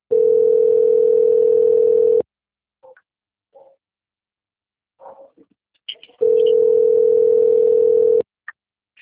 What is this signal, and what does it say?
Ringback tone (North America), 440 + 480 Hz.
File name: out- - - - - .40893